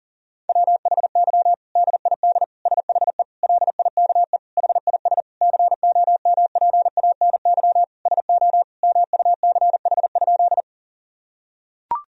Morse code, heard as a keyed tone, about 37 wpm, 700 Hz. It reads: WHY DID SHE LIKE HIS COMPANY SO MUCH?
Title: Sentences from Top 1000 Words - Review Part 3 - 40wpm